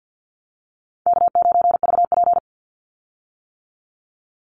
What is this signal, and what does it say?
Morse code, heard as a keyed tone, about 48 wpm, 700 Hz. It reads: K9VP
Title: Easy Call Signs - 50wpm